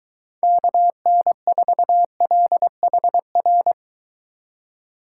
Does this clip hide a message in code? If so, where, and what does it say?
Yes: Morse code from 0.4 to 3.7 s, heard as a keyed tone, about 22 wpm, 700 Hz. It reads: KN4LHR